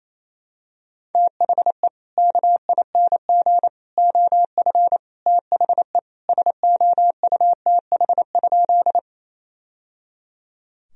Morse code, heard as a keyed tone, about 27 wpm, 700 Hz. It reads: THE KING OF THE SOUTH?
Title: Sentences from Top 1000 Words - Review Part 2 - 28wpm